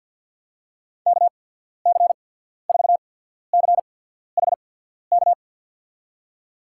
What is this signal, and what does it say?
Morse code, heard as a keyed tone, about 53 wpm, spaced as for 14 wpm, 700 Hz. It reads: KC4CHK
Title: USA Call Signs - 6 Characters - 8x Character Spacing - 50wpm